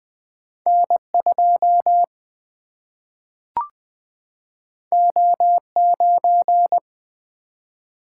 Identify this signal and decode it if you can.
Morse code, heard as a keyed tone, about 20 wpm, 700 Hz. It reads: N2 O9
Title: Single-Letter Single-Number 20wpm